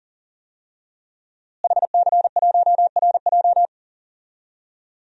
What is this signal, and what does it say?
Morse code, heard as a keyed tone, about 39 wpm, 700 Hz. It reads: HC1RJ
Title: DXCC-100 Call Signs - 5 Characters - Repeated 3x - 40wpm